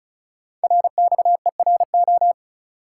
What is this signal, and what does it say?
Morse code, heard as a keyed tone, about 34 wpm, 700 Hz. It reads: RXERO
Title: Code Groups - 5 Letters - 35wpm